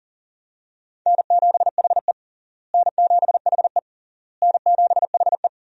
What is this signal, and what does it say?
Morse code, heard as a keyed tone, about 40 wpm, 700 Hz. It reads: N7HE N7HE N7HE
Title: USA Call Signs - 4 Characters - Repeated 3x - 40wpm